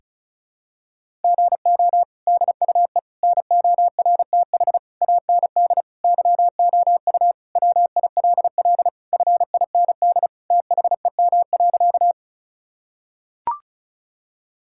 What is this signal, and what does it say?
Morse code, heard as a keyed tone, about 34 wpm, 700 Hz. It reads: GO DUE NORTH AND YOU WILL FIND THEM.
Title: Sentences from Top 1000 Words - Review Part 6 - 35wpm